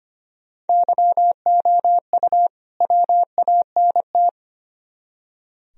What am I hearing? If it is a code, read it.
Morse code, heard as a keyed tone, about 33 wpm, 700 Hz. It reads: YOU WANT